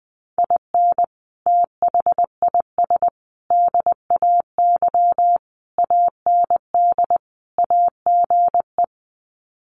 Morse code, heard as a keyed tone, about 20 wpm, 700 Hz. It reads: IN THIS DAY AND AGE
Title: Sets of 5 Words Encore 20wpm